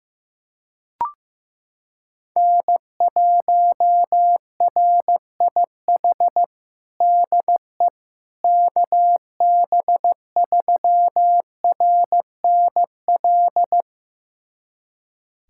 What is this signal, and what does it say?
Morse code, heard as a keyed tone, about 15 wpm, 700 Hz. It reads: N1RIH DE KB3RNL
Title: USA Callsign DE Callsign 15wpm